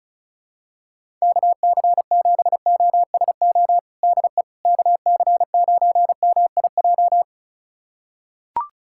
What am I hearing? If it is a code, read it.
Morse code, heard as a keyed tone, about 36 wpm, 700 Hz. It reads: KC7OSO DE KC9MIJ
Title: USA Callsign DE Callsign - 6 Characters - 35wpm